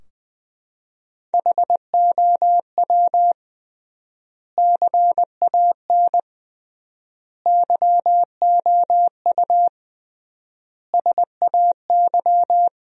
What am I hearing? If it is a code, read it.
Morse code, heard as a keyed tone, about 20 wpm, 700 Hz. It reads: HOW CAN YOU SAY
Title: 3x Word-Spacing - Sentences from Top 200 Words - 20wpm